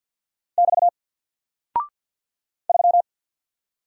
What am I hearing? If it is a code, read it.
Morse code, heard as a keyed tone, about 52 wpm, 700 Hz. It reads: <BT> 3
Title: Single Character (Letter:Number:Punctuation:Prosign) Alt 50wpm